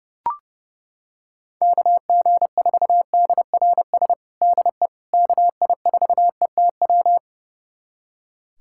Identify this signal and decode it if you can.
Morse code, heard as a keyed tone, about 30 wpm, 700 Hz. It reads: KG4DRS DE KI4ETW
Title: USA Callsign DE Callsign - 6 Characters - 30wpm